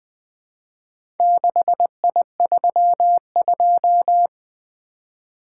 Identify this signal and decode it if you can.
Morse code, heard as a keyed tone, about 20 wpm, 700 Hz. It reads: 6I32
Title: Four Letter-Number Combinations - 20wpm